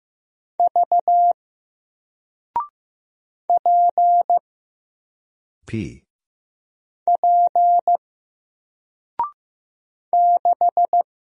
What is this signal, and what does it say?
Morse code, heard as a keyed tone, about 15 wpm, 700 Hz. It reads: V P P 6
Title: Single Letter-Number Alt 15wpm